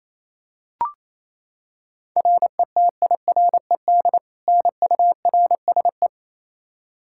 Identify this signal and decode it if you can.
Morse code, heard as a keyed tone, about 28 wpm, 700 Hz. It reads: RETIRED NURSE
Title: Practice QSO - Exchange 3 - More Info - 28wpm